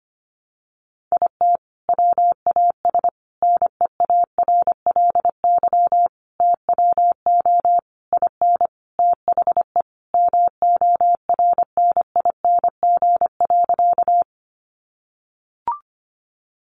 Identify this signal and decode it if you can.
Morse code, heard as a keyed tone, about 25 wpm, 700 Hz. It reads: IT WAS NEARLY TWO IN THE MORNING.
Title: Sentences from Top 1300 Words 25wpm